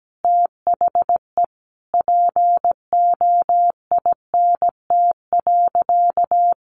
Morse code, heard as a keyed tone, about 17 wpm, 700 Hz. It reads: THE POINT.
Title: Sentences from Top 200 Words 17wpm